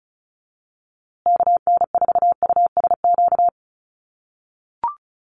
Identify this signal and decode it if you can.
Morse code, heard as a keyed tone, about 34 wpm, 700 Hz. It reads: KN4USQ
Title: USA Call Signs Repeated 3x 35wpm